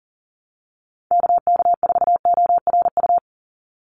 Morse code, heard as a keyed tone, about 40 wpm, 700 Hz. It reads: KK4ORU
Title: USA Call Signs 40wpm